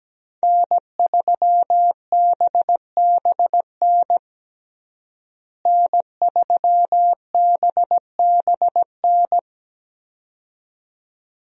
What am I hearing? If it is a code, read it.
Morse code, heard as a keyed tone, about 17 wpm, 700 Hz. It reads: N3BBN N3BBN
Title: USA Call Signs - 5 Characters - Repeated 2x - 17wpm